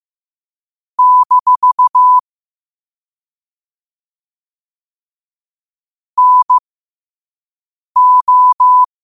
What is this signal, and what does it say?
Morse code, heard as a keyed tone, about 15 wpm, spaced as for 5 wpm, 1000 Hz. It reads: - NO
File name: qotdfiles / 5wpm